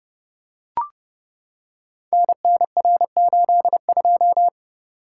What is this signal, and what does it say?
Morse code, heard as a keyed tone, about 30 wpm, 700 Hz. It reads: NNR82
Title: Code Groups - 5 Letter:Numbers - 30wpm